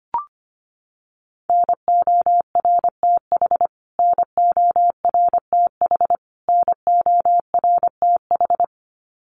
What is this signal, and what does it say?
Morse code, heard as a keyed tone, about 25 wpm, 700 Hz. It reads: NORTH NORTH NORTH
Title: Top 500 Words Repeated 25wpm